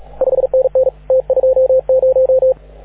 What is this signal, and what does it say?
Morse code, heard as a keyed tone, about 35 wpm, 530 Hz, with radio noise. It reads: 5NN T20
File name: AUDIO2019 / HG6N CW SSB MIX